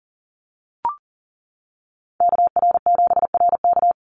Morse code, heard as a keyed tone, about 40 wpm, 700 Hz. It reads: KR7RK
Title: NA SOTA Chaser Callsigns 40wpm